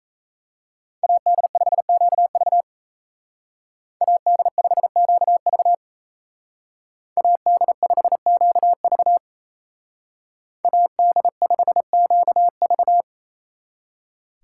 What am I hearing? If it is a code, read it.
Morse code, heard as a keyed tone, about 33 wpm, 700 Hz. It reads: AD5QV AD5QV AD5QV AD5QV
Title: Speed Racing - USA 2x2 Call Signs 28wpm